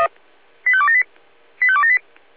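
Five-tone selective call (ZVEI), heard as quick runs of five tones, tones of about 70 ms. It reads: 85278 85278
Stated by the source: ZVEI five-tone call received by SDR.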